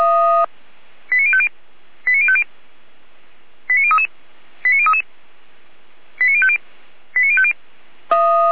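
Five-tone selective call (ZVEI), heard as quick runs of five tones, tones of about 70 ms. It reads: 89050 89050 89033 89033 89050 89050
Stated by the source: ZVEI five-tone selective call received on an SDR.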